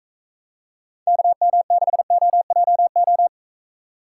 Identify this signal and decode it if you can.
Morse code, heard as a keyed tone, about 44 wpm, 700 Hz. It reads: KM6OJO